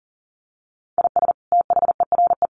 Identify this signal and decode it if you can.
Morse code, heard as a keyed tone, about 40 wpm, 700 Hz. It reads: IS THERE
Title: Sentences from Top 300 Words 40wpm